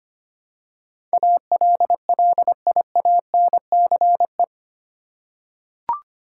Morse code, heard as a keyed tone, about 25 wpm, 700 Hz. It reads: ALLIANCE
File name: 8 Letter Words - First 3 Letters Given - 25wpm